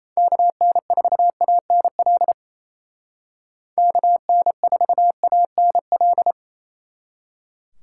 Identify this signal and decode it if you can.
Morse code, heard as a keyed tone, about 35 wpm, 700 Hz. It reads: KN4ANL KN4ANL
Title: Speed Racing - USA 2x3 Call Signs 28wpm